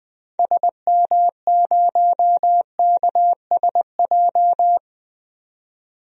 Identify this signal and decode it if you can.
Morse code, heard as a keyed tone, about 20 wpm, 700 Hz. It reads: SM0KSJ
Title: DXCC-100 Call Signs - 6 Characters - 20wpm